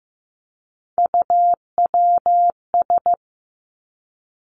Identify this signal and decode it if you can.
Morse code, heard as a keyed tone, about 15 wpm, 700 Hz. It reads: UWS
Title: Three Challenging Letter-Number Combinations 15wpm